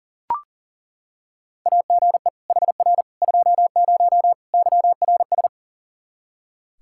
Morse code, heard as a keyed tone, about 40 wpm, 700 Hz. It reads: AGE HR 20 YRS
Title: Practice QSO - Exchange 3 - More Info - 40wpm